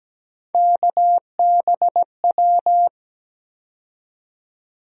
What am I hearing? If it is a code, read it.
Morse code, heard as a keyed tone, about 17 wpm, 700 Hz. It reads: KBW
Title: USA Call Sign Postfixes - 3 Letters 17wpm